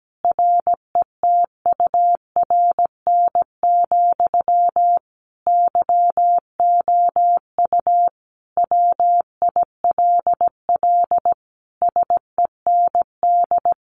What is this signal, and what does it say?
Morse code, heard as a keyed tone, about 17 wpm, 700 Hz. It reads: RETURN, YOU WILL SEND
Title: Sentences from Top 1500 Words 17wpm